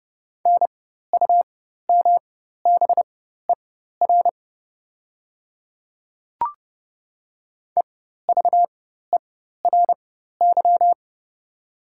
Morse code, heard as a keyed tone, about 31 wpm, spaced as for 14 wpm, 700 Hz. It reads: NUMBER EVERY
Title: 4x Character Spacing - Top 300 Words - 30wpm